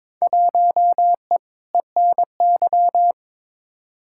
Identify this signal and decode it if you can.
Morse code, heard as a keyed tone, about 22 wpm, 700 Hz. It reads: 1E ENY
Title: Field Day Exchanges - 22wpm